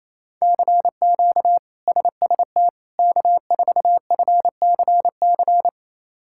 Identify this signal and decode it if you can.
Morse code, heard as a keyed tone, about 28 wpm, 700 Hz. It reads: CQ SST K4FCC
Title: K1USN SST QSO Practice - 28wpm